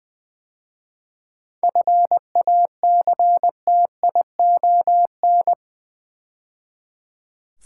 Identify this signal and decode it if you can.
Morse code, heard as a keyed tone, about 20 wpm, 700 Hz. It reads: FACTION
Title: 7 Letter Words - First 3 Letters Given - 20wpm